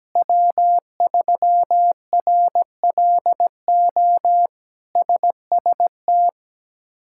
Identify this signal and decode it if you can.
Morse code, heard as a keyed tone, about 17 wpm, 700 Hz. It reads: W3RLO SST
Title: K1USN SST QSO Practice - 17wpm